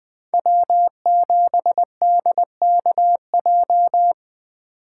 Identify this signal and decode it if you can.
Morse code, heard as a keyed tone, about 25 wpm, 700 Hz. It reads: W7DKJ
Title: Speed Racing - USA 1x3 Call Signs 20wpm